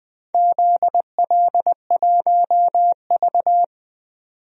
Morse code, heard as a keyed tone, about 20 wpm, 700 Hz. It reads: ZL1V
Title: DXCC-100 Call Signs - 4 Characters - 20wpm